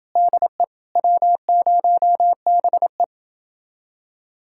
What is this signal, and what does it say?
Morse code, heard as a keyed tone, about 29 wpm, 700 Hz. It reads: DE W0BE